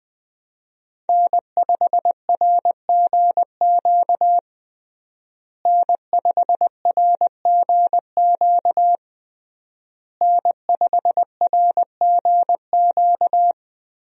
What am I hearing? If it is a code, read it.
Morse code, heard as a keyed tone, about 20 wpm, 700 Hz. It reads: N5RGQ N5RGQ N5RGQ
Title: USA Call Signs - 5 Characters - Repeated 3x - 20wpm